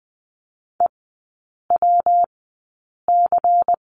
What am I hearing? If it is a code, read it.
Morse code, heard as a keyed tone, about 20 wpm, spaced as for 8 wpm, 700 Hz. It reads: EWC
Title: 166 Lesson 27e - Number 7 - Three Characters 20wpm